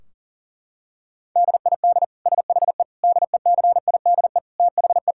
Morse code, heard as a keyed tone, about 37 wpm, 700 Hz. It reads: DID SHE DECIDE THE
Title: Sentences from Top 1600 Words 40wpm